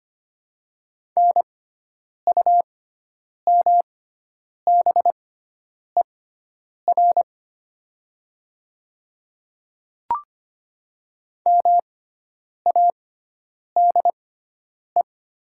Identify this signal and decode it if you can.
Morse code, heard as a keyed tone, about 25 wpm, spaced as for 9 wpm, 700 Hz. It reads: NUMBER MADE
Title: 6x Character Spacing - Top 200 Words - 25wpm